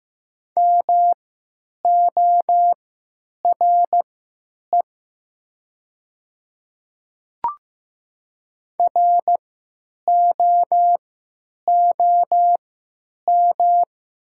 Morse code, heard as a keyed tone, about 15 wpm, spaced as for 9 wpm, 690 Hz. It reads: MORE ROOM
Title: 3x Character Spacing - Top 300 Words - 15wpm